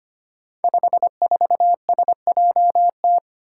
Morse code, heard as a keyed tone, about 25 wpm, 700 Hz. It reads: 54SJT
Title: Five Challenging Letter-Number Combinations 25wpm